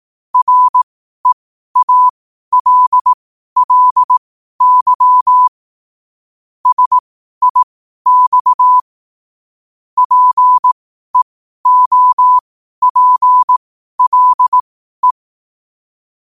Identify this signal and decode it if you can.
Morse code, heard as a keyed tone, about 18 wpm, spaced as for 12 wpm, 1000 Hz. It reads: REALLY SIX PEOPLE
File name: qotdfiles / 12wpm